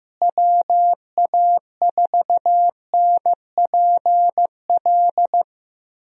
Morse code, heard as a keyed tone, about 15 wpm, 700 Hz. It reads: WA4NPL
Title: USA Call Signs - 6 Characters - Repeated 3x - 15wpm